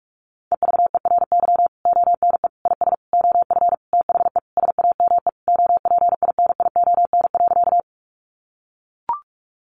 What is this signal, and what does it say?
Morse code, heard as a keyed tone, about 44 wpm, 700 Hz. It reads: EVERY ONE IS OF THE SAME OPINION.
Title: Sentences from Top 1200 Words 45wpm